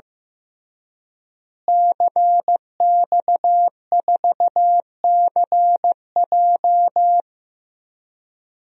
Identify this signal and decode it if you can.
Morse code, heard as a keyed tone, about 15 wpm, 700 Hz. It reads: CX4CJ
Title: DXCC-100 Call Signs - 5 Characters - Repeated 3x - 15wpm